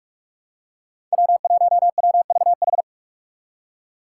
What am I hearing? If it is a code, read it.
Morse code, heard as a keyed tone, about 44 wpm, 700 Hz. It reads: W1WVH
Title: Five Challenging Letter-Number Combinations 45wpm